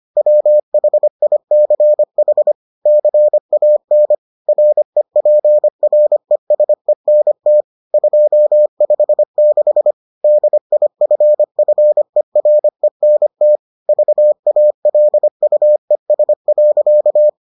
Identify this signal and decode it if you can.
Morse code, heard as a keyed tone, about 25 wpm, 600 Hz. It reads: WHICH CAN REPRESENT 256 DIFFERENT VALUES.